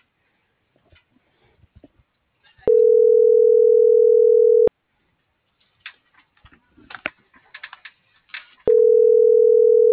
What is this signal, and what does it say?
Ringback tone (North America), 440 + 480 Hz.